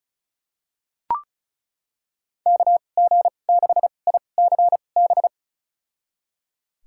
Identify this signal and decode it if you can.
Morse code, heard as a keyed tone, about 34 wpm, spaced as for 25 wpm, 690 Hz. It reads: KG6ICB
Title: USA Call Signs - 6 Characters - 2x Character Spacing - 35wpm